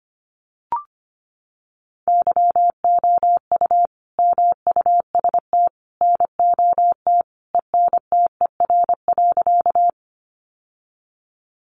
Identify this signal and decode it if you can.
Morse code, heard as a keyed tone, about 25 wpm, 700 Hz. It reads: YOU MUST NOT ENTER.
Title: Sentences from Top 1300 Words 25wpm